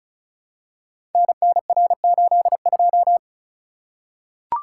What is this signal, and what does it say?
Morse code, heard as a keyed tone, about 36 wpm, 700 Hz. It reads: NNR82
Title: Code Groups - 5 Letter:Numbers - 35wpm